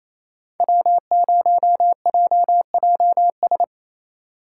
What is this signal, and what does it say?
Morse code, heard as a keyed tone, about 33 wpm, 700 Hz. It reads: W0JJS
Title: Speed Racing - USA 1x3 Call Signs 28wpm